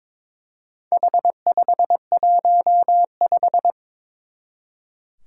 Morse code, heard as a keyed tone, about 22 wpm, 700 Hz. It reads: H515
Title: Four Challenging Letter-Number Combinations 22wpm